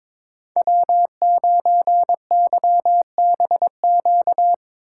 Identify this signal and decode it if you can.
Morse code, heard as a keyed tone, about 22 wpm, 700 Hz. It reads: W9YBQ
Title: USA Call Signs - 6x Character Spacing - 22wpm